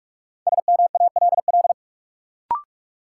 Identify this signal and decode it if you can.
Morse code, heard as a keyed tone, about 46 wpm, 700 Hz. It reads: SMALL
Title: Sentences from Top 200 Words - New Word Drill - 45wpm